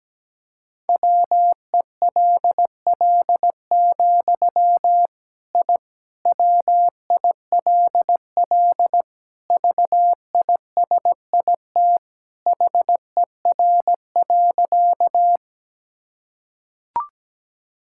Morse code, heard as a keyed tone, about 17 wpm, 700 Hz. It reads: WELL, I WILL VISIT HER.
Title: Sentences from Top 1000 Words - Review Part 6 - 17wpm